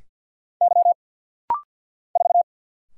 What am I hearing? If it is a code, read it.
Morse code, heard as a keyed tone, about 50 wpm, 690 Hz. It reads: Y 4
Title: Single Letter-Number Alt - Rapid-Fire 50wpm